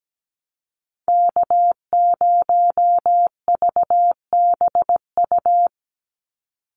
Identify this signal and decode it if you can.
Morse code, heard as a keyed tone, about 17 wpm, 700 Hz. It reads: K0VBU
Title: Call Signs - 17wpm